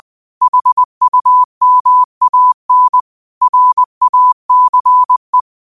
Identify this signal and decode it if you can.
Morse code, heard as a keyed tone, about 20 wpm, 1000 Hz. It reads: HUMAN RACE
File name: qotdfiles / 20wpm